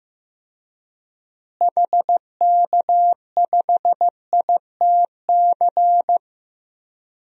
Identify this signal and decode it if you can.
Morse code, heard as a keyed tone, about 16 wpm, 700 Hz. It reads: HK5ITC
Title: DXCC-100 Call Signs - 6 Characters - Repeated 2x - 15wpm